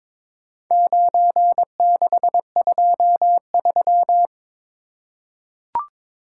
Morse code, heard as a keyed tone, about 22 wpm, 700 Hz. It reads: 9623
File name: practice / 22wpm / Four Letter-Number Combinations - 22wpm